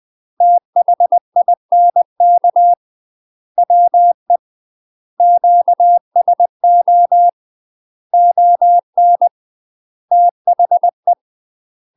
Morse code, heard as a keyed tone, about 20 wpm, 700 Hz. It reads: THINK WE QSO ON THE
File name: UK MASTER QSOs TRAINING/20WPM